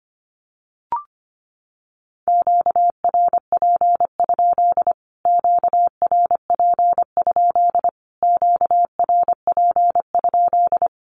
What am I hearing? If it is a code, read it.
Morse code, heard as a keyed tone, about 25 wpm, 700 Hz. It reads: QRP? QRP? QRP?
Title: Q-Codes Repeated 3x 25wpm